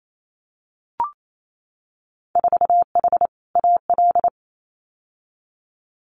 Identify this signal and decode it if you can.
Morse code, heard as a keyed tone, about 28 wpm, 700 Hz. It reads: 4H AL
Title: Winter Field Day Exchanges Encore 28wpm